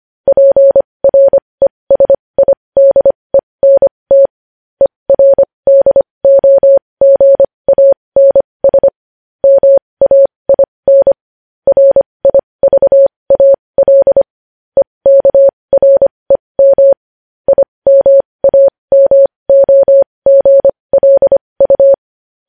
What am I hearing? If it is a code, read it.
Morse code, heard as a keyed tone, about 25 wpm, spaced as for 20 wpm, 550 Hz. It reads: PRESIDENT ERDOGANS MAIN RIVAL EKREM IMAMOGLU
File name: podcast / News headlines-1x-25x20